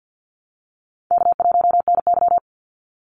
Morse code, heard as a keyed tone, about 52 wpm, 700 Hz. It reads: K1NY
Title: Easy Call Signs Repeated 3x - 50wpm